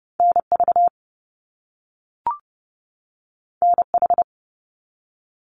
Morse code, heard as a keyed tone, about 30 wpm, 700 Hz. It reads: NV NH
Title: US States 30wpm